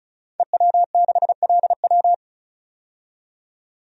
Morse code, heard as a keyed tone, about 36 wpm, 700 Hz. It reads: EW6LW
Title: DXCC-100 Call Signs - 5 Characters - 35wpm